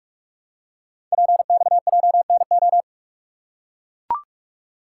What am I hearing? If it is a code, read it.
Morse code, heard as a keyed tone, about 44 wpm, 700 Hz. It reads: PXJNO
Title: Code Groups - 5 Letters - 45wpm